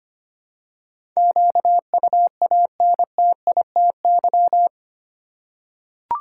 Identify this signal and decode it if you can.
Morse code, heard as a keyed tone, about 25 wpm, 700 Hz. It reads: QUANTITY
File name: 8 Letter Words - First 3 Letters Given - 25wpm